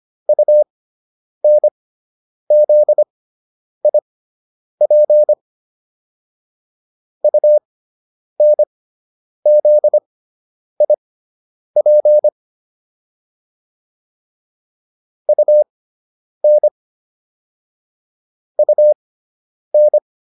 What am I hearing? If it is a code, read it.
Morse code, heard as a keyed tone, about 25 wpm, spaced as for 9 wpm, 600 Hz. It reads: UNZIP UNZIP UN UN